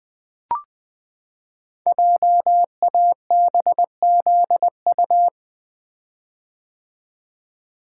Morse code, heard as a keyed tone, about 20 wpm, 700 Hz. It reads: JABZU
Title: Code Groups - 5 Letters - 20wpm